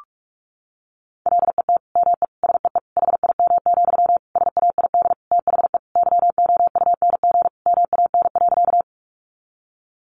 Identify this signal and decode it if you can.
Morse code, heard as a keyed tone, about 44 wpm, 700 Hz. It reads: LET ME SEE HIM, SAID THE YOUNG MAN.
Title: Sentences from Top 400 Words 45wpm